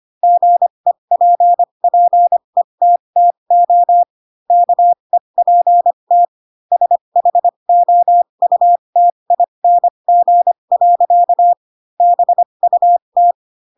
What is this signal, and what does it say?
Morse code, heard as a keyed tone, about 25 wpm, spaced as for 22 wpm, 700 Hz. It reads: GEPPETTO KEPT SHOUTING. BUT
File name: Pinocchio / Pinocchio ch3 18 25 ik1hge